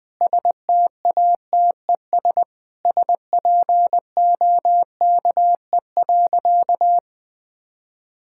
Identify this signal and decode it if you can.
Morse code, heard as a keyed tone, about 20 wpm, 700 Hz. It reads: STATES SPOKE.
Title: Sentences with 4-Word Phrases - Spoken Beginning - 20wpm